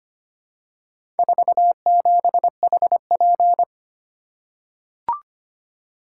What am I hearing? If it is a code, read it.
Morse code, heard as a keyed tone, about 25 wpm, 700 Hz. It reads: 47HP
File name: practice / 25wpm / Four Letter-Number Combinations - 25wpm